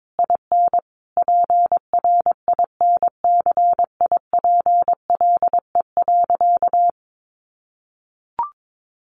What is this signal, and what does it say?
Morse code, heard as a keyed tone, about 22 wpm, 700 Hz. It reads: IN PRINCIPLE.
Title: Sentences from Top 1400 Words 22wpm